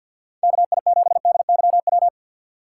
Morse code, heard as a keyed tone, about 47 wpm, 700 Hz. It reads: KI7DYW
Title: USA Call Signs - 3x Character Spacing - 50wpm